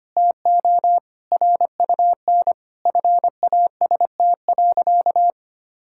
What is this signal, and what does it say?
Morse code, heard as a keyed tone, about 25 wpm, 700 Hz. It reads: TO RUN FAST.
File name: practice / 25wpm / Sentences with 3-Word Phrases - 25wpm